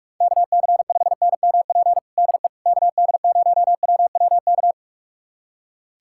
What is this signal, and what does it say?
Morse code, heard as a keyed tone, about 46 wpm, 700 Hz. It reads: KC5NMP DE KD0WWK
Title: USA Callsign DE Callsign 45wpm